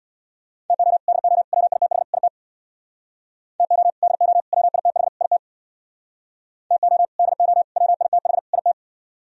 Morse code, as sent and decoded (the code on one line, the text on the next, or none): - --- / -.. --- / .-- .. - .... / .. - / - --- / -.. --- / .-- .. - .... / .. - / - --- / -.. --- / .-- .. - .... / .. -
TO DO WITH IT TO DO WITH IT TO DO WITH IT